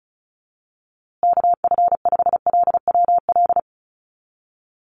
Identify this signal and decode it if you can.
Morse code, heard as a keyed tone, about 34 wpm, 700 Hz. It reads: KF5LWL
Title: USA Call Signs Repeated 2x 35wpm